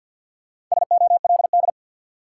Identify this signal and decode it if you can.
Morse code, heard as a keyed tone, about 53 wpm, 700 Hz. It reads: SOLD
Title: Sentences from Top 1400 Words - New Word Drill Alt - 50wpm